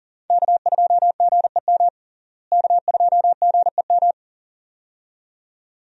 Morse code, heard as a keyed tone, about 40 wpm, 700 Hz. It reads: K2GEM K2GEM
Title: USA Call Signs - 5 Characters - Repeated 2x - 40wpm